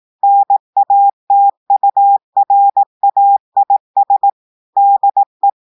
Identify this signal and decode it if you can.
Morse code, heard as a keyed tone, about 18 wpm, 800 Hz. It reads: NATURAIS DE
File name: audios / portugais / portugais texte en clair 2 vitesse 18 mots minutes